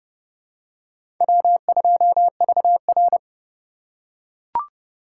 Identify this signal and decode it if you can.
Morse code, heard as a keyed tone, about 32 wpm, 700 Hz. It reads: W2VR
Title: USA Call Signs - 4 Characters - 6x Character Spacing - 30wpm